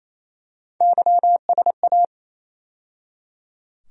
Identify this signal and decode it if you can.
Morse code, heard as a keyed tone, about 27 wpm, 700 Hz. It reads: YSA
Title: USA Call Sign Postfixes - 3 Letters 28wpm